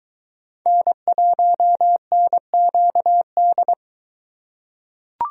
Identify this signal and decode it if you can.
Morse code, heard as a keyed tone, about 22 wpm, 700 Hz. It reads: N1NQD